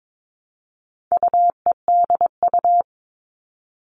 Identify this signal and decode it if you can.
Morse code, heard as a keyed tone, about 22 wpm, 700 Hz. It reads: UEDU
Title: All Cut Numbers - 4-Digit 22wpm